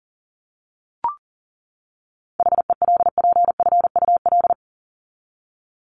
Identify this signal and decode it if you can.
Morse code, heard as a keyed tone, about 38 wpm, 700 Hz. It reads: HELPFUL
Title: Top 751-1000 Words 40wpm